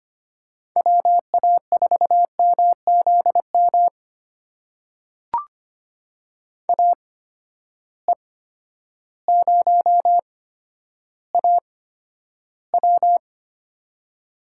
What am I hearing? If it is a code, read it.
Morse code, heard as a keyed tone, about 25 wpm, spaced as for 11 wpm, 700 Hz. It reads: WA4MZM AE0AW
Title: USA Call Signs - 8x Character Spacing - 25wpm